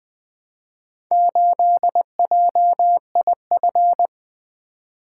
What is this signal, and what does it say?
Morse code, heard as a keyed tone, about 20 wpm, 700 Hz. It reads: 8JIF
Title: Four Letter-Number Combinations - 20wpm